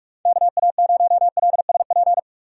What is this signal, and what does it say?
Morse code, heard as a keyed tone, about 42 wpm, 700 Hz. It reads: KA0LSP
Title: USA Call Signs - 6 Characters - 45wpm